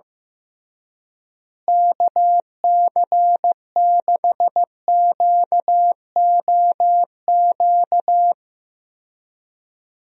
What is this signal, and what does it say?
Morse code, heard as a keyed tone, about 15 wpm, 700 Hz. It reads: KC6QOQ
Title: USA Call Signs - 6 Characters - Repeated 3x - 15wpm